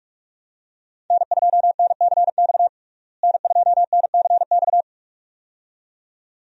Morse code, heard as a keyed tone, about 43 wpm, 700 Hz. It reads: N2NCX N2NCX
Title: USA Call Signs - 5 Characters - Repeated 2x - 45wpm